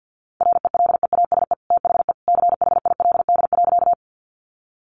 Morse code, heard as a keyed tone, about 48 wpm, 700 Hz. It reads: RELEASE THE CHILD.
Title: Sentences from Top 1100 Words 50wpm